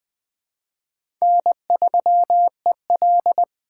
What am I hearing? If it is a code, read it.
Morse code, heard as a keyed tone, about 20 wpm, 700 Hz. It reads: N3EL
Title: Speed Racing - USA 1x2 Call Signs 17wpm